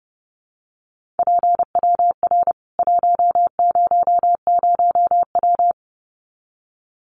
Morse code, heard as a keyed tone, about 30 wpm, 700 Hz. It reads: PWR 100W
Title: Common QSO Phrases 30wpm